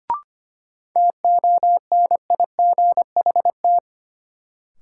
Morse code, heard as a keyed tone, about 25 wpm, 700 Hz. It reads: TONIGHT
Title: Sentences from Top 600 Words - New Word Drill - 25wpm